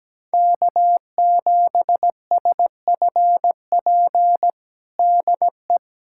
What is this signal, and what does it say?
Morse code, heard as a keyed tone, about 17 wpm, 700 Hz. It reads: K7SFP DE
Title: USA Callsign DE Callsign 17wpm